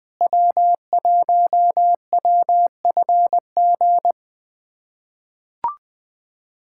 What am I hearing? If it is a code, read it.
Morse code, heard as a keyed tone, about 20 wpm, 700 Hz. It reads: W1WFG
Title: USA Callsign DE Callsign - 5 Characters - 20wpm